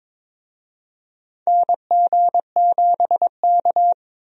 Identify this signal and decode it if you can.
Morse code, heard as a keyed tone, about 22 wpm, 700 Hz. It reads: NG7K
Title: USA Call Signs - 4 Characters - Repeated 2x - 22wpm